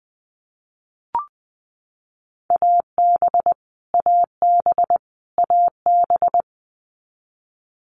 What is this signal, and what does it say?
Morse code, heard as a keyed tone, about 20 wpm, 700 Hz. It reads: AB AB AB
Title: Canadian Provinces and Territories - Repeated 3x - 20wpm